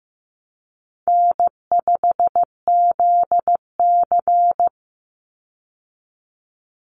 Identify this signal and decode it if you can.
Morse code, heard as a keyed tone, about 15 wpm, 700 Hz. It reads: N5ZC
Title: Call Signs Repeated 3x - 15wpm